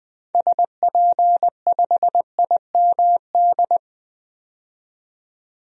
Morse code, heard as a keyed tone, about 21 wpm, 700 Hz. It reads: SP5IMD
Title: DXCC-100 Call Signs - 6 Characters - 20wpm